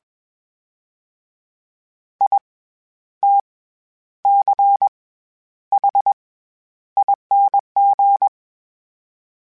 Morse code, heard as a keyed tone, about 20 wpm, 800 Hz. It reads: I T C H ING
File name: Special INGs-mp3s / Special INGs-mp3s / SlowToFaster ing @20